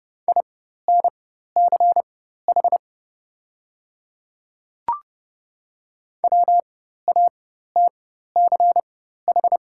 Morse code, heard as a keyed tone, about 30 wpm, spaced as for 14 wpm, 700 Hz. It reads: INCH WATCH
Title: 4x Character Spacing - Top 500 Words - 30wpm